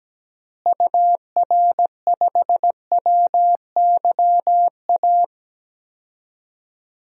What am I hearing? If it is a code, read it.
Morse code, heard as a keyed tone, about 17 wpm, 700 Hz. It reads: UR5WYA
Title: DXCC-100 Call Signs - 6 Characters - Repeated 2x - 17wpm